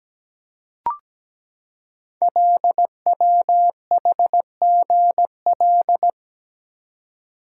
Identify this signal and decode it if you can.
Morse code, heard as a keyed tone, about 17 wpm, 700 Hz. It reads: LWHGL
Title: Code Groups - 5 Letter:Numbers - 17wpm